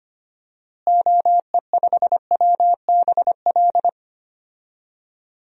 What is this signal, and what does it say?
Morse code, heard as a keyed tone, about 25 wpm, 700 Hz. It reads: OE5WBL
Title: DXCC-100 Call Signs - 6 Characters - Repeated 2x - 25wpm